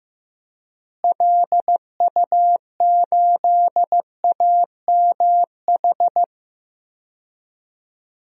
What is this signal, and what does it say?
Morse code, heard as a keyed tone, about 15 wpm, 700 Hz. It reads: LU8AMH
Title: DXCC-100 Call Signs - 6 Characters - Repeated 2x - 15wpm